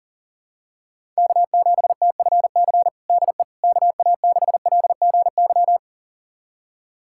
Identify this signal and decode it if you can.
Morse code, heard as a keyed tone, about 40 wpm, 700 Hz. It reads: K7TFC DE KA6LGY
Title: USA Callsign DE Callsign 40wpm